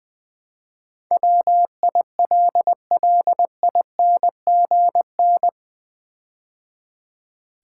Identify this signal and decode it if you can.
Morse code, heard as a keyed tone, about 20 wpm, 700 Hz. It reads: WILLINGN
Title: Build A Word - 4 Letters At A Time - 11-Letter Words - 20wpm